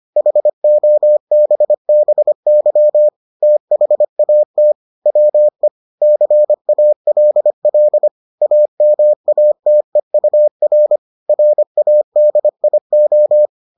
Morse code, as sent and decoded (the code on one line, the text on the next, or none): .... --- -... -... -.-- / - .... .- - / .-- . / -.-. .- .-.. .-.. / .- -- .- - . ..- .-. / .-. .- -.. .. ---
HOBBY THAT WE CALL AMATEUR RADIO